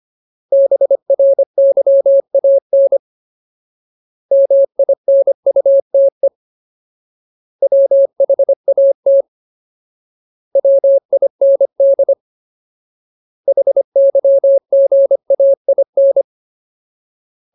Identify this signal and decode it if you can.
Morse code, heard as a keyed tone, about 25 wpm, 550 Hz. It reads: BRYAN MINUTE WHAT WIND HYGAIN
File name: Ultimate Word List / 25WPM Ultimate